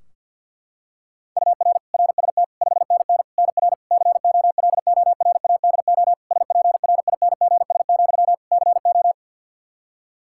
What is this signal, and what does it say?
Morse code, heard as a keyed tone, about 49 wpm, 700 Hz. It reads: UR RST 5NN NR COLORADO SPRINGS, CO
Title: Practice QSO - Exchange 1 - Beginning - 50wpm